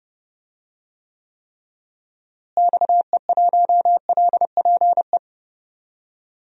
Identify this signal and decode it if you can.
Morse code, heard as a keyed tone, about 30 wpm, 700 Hz. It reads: XE1LPE
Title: DXCC-100 Call Signs - 6 Characters - Repeated 3x - 30wpm